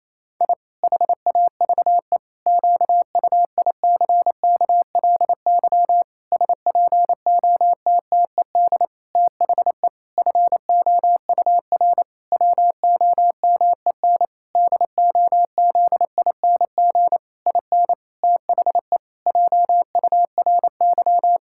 Morse code, heard as a keyed tone, about 28 wpm, 700 Hz. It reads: I HAVE QUICKLY SPOTTED THE FOUR WOMEN DOZING IN THE JURY
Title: Pangram Challenge v2 - 28wpm